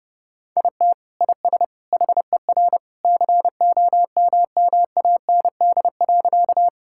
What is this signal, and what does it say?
Morse code, heard as a keyed tone, about 32 wpm, 700 Hz. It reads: IT IS HER COMMAND.
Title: Sentences from Top 1600 Words 30wpm